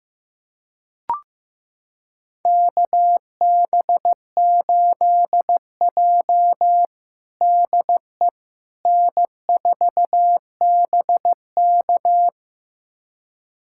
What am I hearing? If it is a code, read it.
Morse code, heard as a keyed tone, about 15 wpm, 700 Hz. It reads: KB8J DE N4BK
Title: USA Callsign DE Callsign - 4 Characters - 15wpm